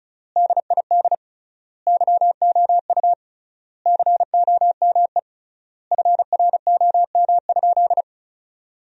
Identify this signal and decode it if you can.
Morse code, heard as a keyed tone, about 34 wpm, 700 Hz. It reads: DID YOU COME FROM?
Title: 3x Word-Spacing - Sentences from Top 200 Words - 35wpm